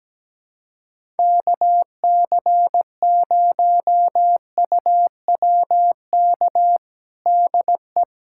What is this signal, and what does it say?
Morse code, heard as a keyed tone, about 17 wpm, 700 Hz. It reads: KC0UWK DE
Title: USA Callsign DE Callsign - 6 Characters - 17wpm